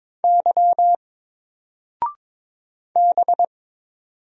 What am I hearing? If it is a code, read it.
Morse code, heard as a keyed tone, about 22 wpm, 700 Hz. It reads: Y B
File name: Single-Letter-Number-22wpm